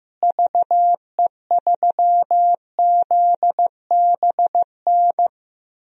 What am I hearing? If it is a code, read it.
Morse code, heard as a keyed tone, about 15 wpm, 700 Hz. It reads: VE3ZBN
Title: DXCC-100 Call Signs - 6 Characters - Repeated 2x - 15wpm